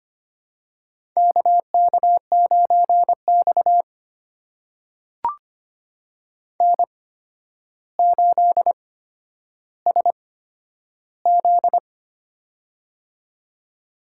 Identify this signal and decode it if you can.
Morse code, heard as a keyed tone, about 25 wpm, spaced as for 11 wpm, 700 Hz. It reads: KK9X N8SZ
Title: USA Call Signs - 4 Characters - 8x Character Spacing - 25wpm